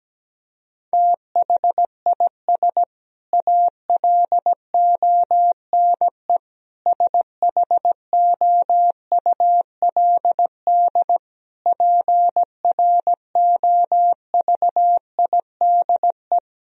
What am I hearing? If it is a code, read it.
Morse code, heard as a keyed tone, about 17 wpm, 700 Hz. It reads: THIS ALONE SHOULD PROVIDE
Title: Sentences from Top 1000 Words - Review Part 4 - 17wpm